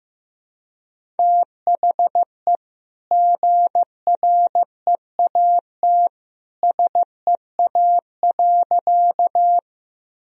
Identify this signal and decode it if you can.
Morse code, heard as a keyed tone, about 15 wpm, 700 Hz. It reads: THE GREAT SEA.
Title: Sentences from Top 1000 Words - Review Part 1 - 15wpm